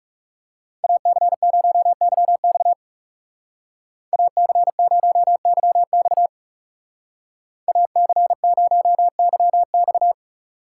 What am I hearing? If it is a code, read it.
Morse code, heard as a keyed tone, about 35 wpm, 700 Hz. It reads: AC0YX AC0YX AC0YX
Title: Speed Racing - USA 2x2 Call Signs 30wpm